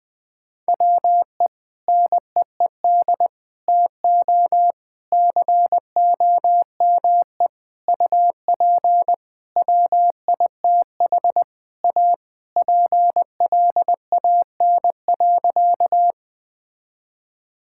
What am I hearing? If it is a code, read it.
Morse code, heard as a keyed tone, about 20 wpm, 700 Hz. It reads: WE NEED TO COME UP WITH A PLAN.
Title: Sentences with 5-Word Phrases - 20wpm